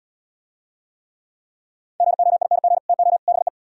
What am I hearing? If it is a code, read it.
Morse code, heard as a keyed tone, about 70 wpm, 700 Hz. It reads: GOING TO BE